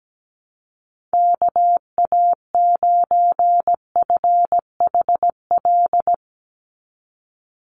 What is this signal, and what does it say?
Morse code, heard as a keyed tone, about 17 wpm, 700 Hz. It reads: KA9FHL
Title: USA Call Signs 17wpm